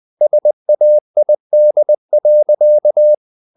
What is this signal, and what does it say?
Morse code, heard as a keyed tone, about 20 wpm, 600 Hz. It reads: SAID.
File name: the old man and the sea / the old man and the sea part 1 20wpm